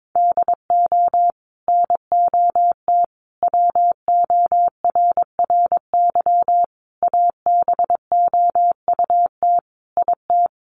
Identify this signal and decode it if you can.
Morse code, heard as a keyed tone, about 22 wpm, 700 Hz. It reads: DO NOT WORRY ABOUT IT
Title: Sets of 5 Words 22wpm